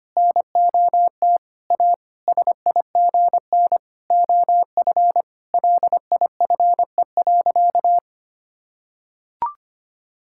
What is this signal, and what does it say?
Morse code, heard as a keyed tone, about 25 wpm, 700 Hz. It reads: NOT A SIGN OF LIFE.
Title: Sentences from Top 1000 Words - Review Part 6 - 25wpm